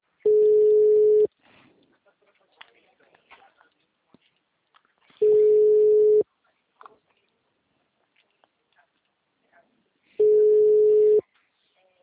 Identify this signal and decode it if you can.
Ringback tone (Europe, 425 Hz), with speech.